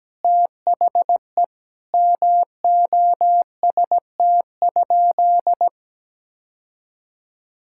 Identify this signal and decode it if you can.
Morse code, heard as a keyed tone, about 17 wpm, 700 Hz. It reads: THE MOST?
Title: Sentences from Top 1000 Words - Review Part 5 - 17wpm